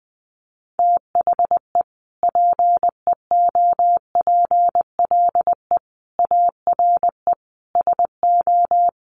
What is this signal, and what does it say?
Morse code, heard as a keyed tone, about 20 wpm, 700 Hz. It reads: THE PEOPLE ARE SO
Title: Sentences from Top 1000 Words 20wpm